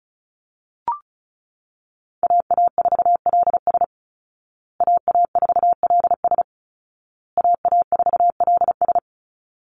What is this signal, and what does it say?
Morse code, heard as a keyed tone, about 34 wpm, 700 Hz. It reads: AA4LS AA4LS AA4LS
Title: Call Signs Repeated 2x - 35wpm